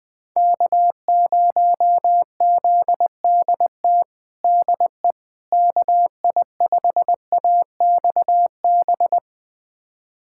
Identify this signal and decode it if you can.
Morse code, heard as a keyed tone, about 20 wpm, 700 Hz. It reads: K0ZDT DE KI5AXB
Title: USA Callsign DE Callsign 20wpm